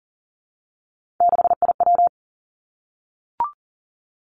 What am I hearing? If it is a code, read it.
Morse code, heard as a keyed tone, about 42 wpm, 700 Hz. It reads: 6IW
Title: Three Challenging Letter-Number Combinations 40wpm